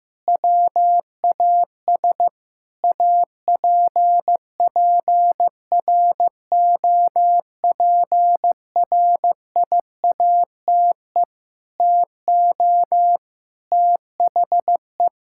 Morse code, heard as a keyed tone, about 16 wpm, 700 Hz. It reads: WAS APPROPRIATE TO THE
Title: Sentences from Top 1000 Words - Review Part 5 - 15wpm